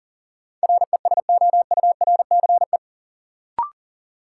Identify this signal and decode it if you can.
Morse code, heard as a keyed tone, about 40 wpm, 700 Hz. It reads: RESOURCE
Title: Sentences from Top 1300 Words - New Word Drill Alt - 40wpm